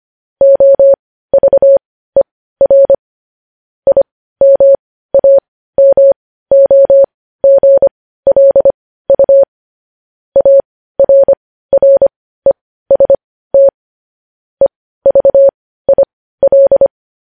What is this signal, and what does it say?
Morse code, heard as a keyed tone, about 25 wpm, spaced as for 15 wpm, 550 Hz. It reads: OVER IMAMOGLU ARREST EVIL
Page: News Headlines in Morse Code at 15 WPM